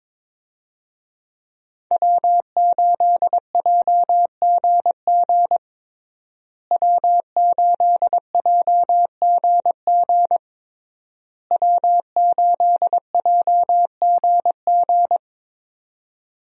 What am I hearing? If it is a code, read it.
Morse code, heard as a keyed tone, about 22 wpm, 700 Hz. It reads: W8JGG W8JGG W8JGG
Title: USA Call Signs - 5 Characters - Repeated 3x - 22wpm